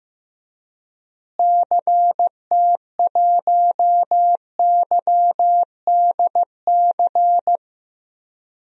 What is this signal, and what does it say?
Morse code, heard as a keyed tone, about 15 wpm, 700 Hz. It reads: CT1YDC
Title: DXCC-100 Call Signs - 6 Characters - Repeated 3x - 15wpm